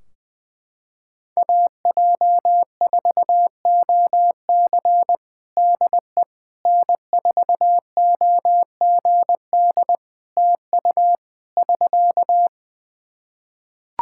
Morse code, heard as a keyed tone, about 21 wpm, 700 Hz. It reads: AJ4OC DE N4OGD TU <SK>
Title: Practice QSO - Exchange 4 - Ending - 20wpm